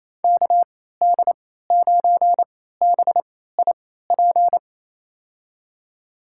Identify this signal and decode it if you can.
Morse code, heard as a keyed tone, about 28 wpm, spaced as for 16 wpm, 700 Hz. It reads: KD9BIP
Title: USA Call Signs - 6 Characters - 3x Character Spacing - 28wpm